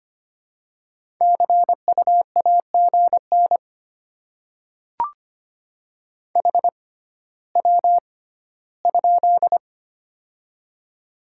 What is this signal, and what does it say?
Morse code, heard as a keyed tone, about 25 wpm, spaced as for 14 wpm, 700 Hz. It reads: CU AGN HW?
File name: Top 20 QSO Elements - 6x Character Spacing - 25wpm